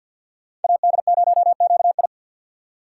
Morse code, heard as a keyed tone, about 50 wpm, 700 Hz. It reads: AD0QI
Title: USA Call Signs - 5 Characters - 6x Character Spacing - 50wpm